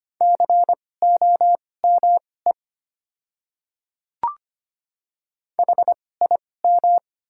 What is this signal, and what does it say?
Morse code, heard as a keyed tone, about 25 wpm, spaced as for 18 wpm, 690 Hz. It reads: COME HIM
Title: 2x Character Spacing - Top 100 Words - 25wpm